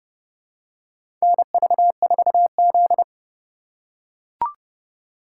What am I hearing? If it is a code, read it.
Morse code, heard as a keyed tone, about 30 wpm, 700 Hz. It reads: NV4Z
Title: USA Call Signs - 3x Character Spacing - 30wpm